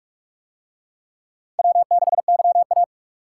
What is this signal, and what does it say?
Morse code, heard as a keyed tone, about 42 wpm, 700 Hz. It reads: W6YA